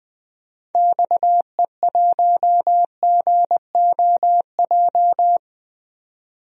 Morse code, heard as a keyed tone, about 20 wpm, 700 Hz. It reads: XE1GOJ
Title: DXCC-100 Call Signs - 6 Characters - 20wpm